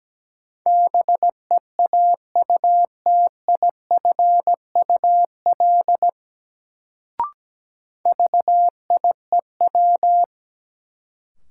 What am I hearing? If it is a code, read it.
Morse code, heard as a keyed tone, about 17 wpm, 700 Hz. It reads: BEAUTIFUL VIEW
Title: Sentences from Top 900 Words - New Word Drill - 17wpm